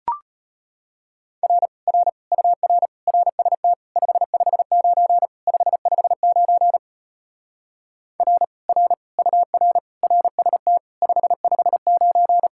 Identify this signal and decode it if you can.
Morse code, heard as a keyed tone, about 33 wpm, 700 Hz. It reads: R R UR RST 559 559 R R UR RST 559